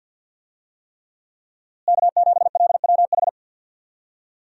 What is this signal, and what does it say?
Morse code, heard as a keyed tone, about 51 wpm, 700 Hz. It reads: K7LWH
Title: USA Call Signs - 5 Characters - Repeated 3x - 50wpm